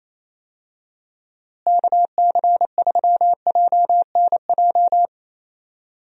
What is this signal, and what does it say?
Morse code, heard as a keyed tone, about 28 wpm, 700 Hz. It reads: KC3JNJ
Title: USA Call Signs - 6 Characters - Repeated 2x - 28wpm